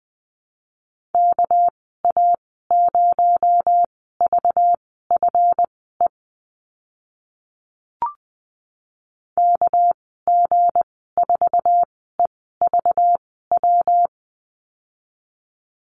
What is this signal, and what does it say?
Morse code, heard as a keyed tone, about 20 wpm, spaced as for 15 wpm, 700 Hz. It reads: KA0VFE KG4EVW
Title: 231 Lesson 37g - Letter x - Call Signs 20wpm